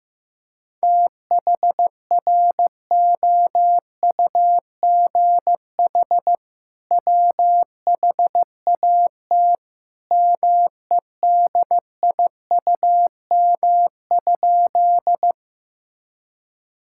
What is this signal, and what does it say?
Morse code, heard as a keyed tone, about 15 wpm, 700 Hz. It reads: THROUGH WHAT MEDIUM?
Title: Sentences from Top 1000 Words - Review Part 6 - 15wpm